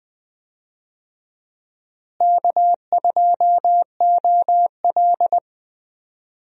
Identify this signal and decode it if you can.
Morse code, heard as a keyed tone, about 20 wpm, 700 Hz. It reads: K2OL
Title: USA Call Signs - 4 Characters - Repeated 3x - 20wpm